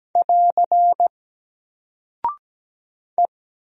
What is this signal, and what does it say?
Morse code, heard as a keyed tone, about 17 wpm, 700 Hz. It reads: <AR> E
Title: Single Character (Letter:Number:Punctuation:Prosign) 17wpm